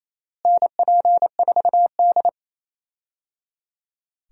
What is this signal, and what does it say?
Morse code, heard as a keyed tone, about 28 wpm, 700 Hz. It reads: NP4D
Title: USA Call Signs - 4 Characters - Repeated 3x - 28wpm